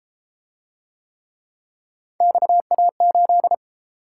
Morse code, heard as a keyed tone, about 32 wpm, 700 Hz. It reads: XA8